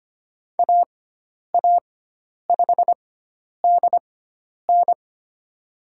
Morse code, heard as a keyed tone, about 25 wpm, spaced as for 10 wpm, 700 Hz. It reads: AA5DN
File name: Lesson 16g - Letter c - Call Signs 25wpm